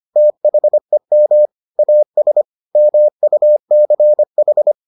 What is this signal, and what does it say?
Morse code, heard as a keyed tone, about 25 wpm, 600 Hz. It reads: THEM AS MUCH